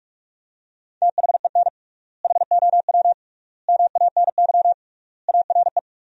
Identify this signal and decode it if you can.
Morse code, heard as a keyed tone, about 46 wpm, 690 Hz. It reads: THEN HOW MANY ARE
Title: 3x Word-Spacing - Sentences from Top 100 Words - 45wpm